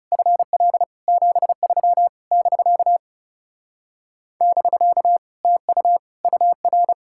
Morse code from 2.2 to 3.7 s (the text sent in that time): <BK>